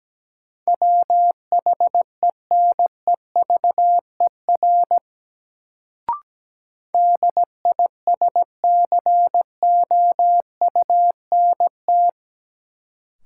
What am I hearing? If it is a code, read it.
Morse code, heard as a keyed tone, about 17 wpm, 700 Hz. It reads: WHENEVER DISCOUNT
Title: Sentences from Top 1200 Words - New Word Drill - 17wpm